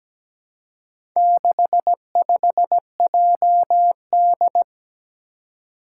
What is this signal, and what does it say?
Morse code, heard as a keyed tone, about 17 wpm, 700 Hz. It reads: 65JD
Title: Four Challenging Letter-Number Combinations 17wpm